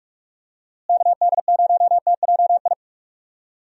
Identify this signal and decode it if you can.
Morse code, heard as a keyed tone, about 46 wpm, 700 Hz. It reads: KD0TJI
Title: USA Call Signs - 6 Characters - Repeated 3x - 45wpm